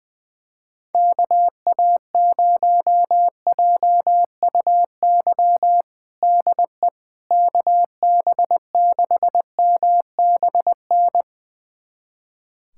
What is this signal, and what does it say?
Morse code, heard as a keyed tone, about 20 wpm, 700 Hz. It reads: KA0JUY DE KB6MBN
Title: USA Callsign DE Callsign 20wpm